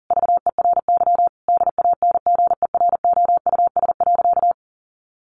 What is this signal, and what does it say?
Morse code, heard as a keyed tone, about 40 wpm, 700 Hz. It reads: VERY DANGEROUS.
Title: Sentences from Top 1200 Words 40wpm